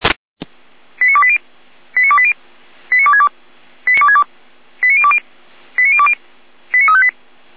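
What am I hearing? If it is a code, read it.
Five-tone selective call (ZVEI), heard as quick runs of five tones, tones of about 70 ms. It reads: 89280 89280 89262 89020 89020 89358
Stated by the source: ZVEI five-tone selective call 89330, received by SDR.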